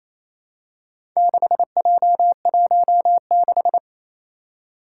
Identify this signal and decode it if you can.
Morse code, heard as a keyed tone, about 27 wpm, 700 Hz. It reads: 6J16
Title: Four Challenging Letter-Number Combinations 28wpm